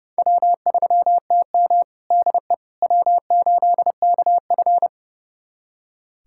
Morse code, heard as a keyed tone, about 32 wpm, 700 Hz. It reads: W3TM DE W8KF
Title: USA Callsign DE Callsign - 4 Characters - 30wpm